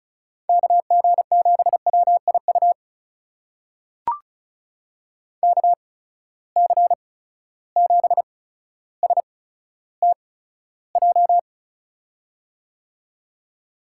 Morse code, heard as a keyed tone, about 36 wpm, spaced as for 15 wpm, 700 Hz. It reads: KG7WIU KC7STJ
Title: USA Call Signs - 6 Characters - 8x Character Spacing - 35wpm